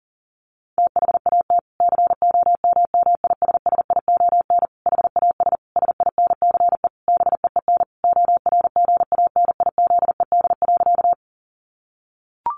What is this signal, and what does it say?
Morse code, heard as a keyed tone, about 40 wpm, 700 Hz. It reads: THAT COMMISSION HAS SINCE BEEN ORGANIZED.
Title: Sentences from Top 1400 Words 40wpm